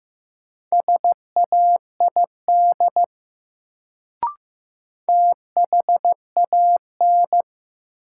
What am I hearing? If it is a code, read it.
Morse code, heard as a keyed tone, about 15 wpm, 700 Hz. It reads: SAID THAN
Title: Sentences from Top 100 Words - New Word Drill - 15wpm